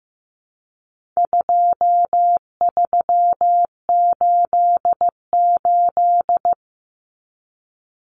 Morse code, heard as a keyed tone, about 15 wpm, 700 Hz. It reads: 2388
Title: Numbers - 4-Digit 15wpm